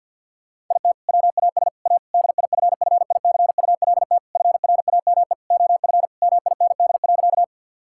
Morse code, heard as a keyed tone, about 50 wpm, 700 Hz. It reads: IT WAS A DIFFICULT FRAME OF MIND.
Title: Sentences from Top 1000 Words - Review Part 6 - 50wpm